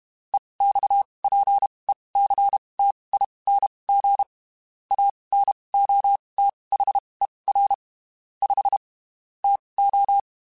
Morse code, heard as a keyed tone, about 32 wpm, spaced as for 23 wpm, 800 Hz. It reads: EXPECTING ANOTHER 5 TO
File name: (spaced) / QSOs Spaced-mp3s / WA7QOX 1 @30 sp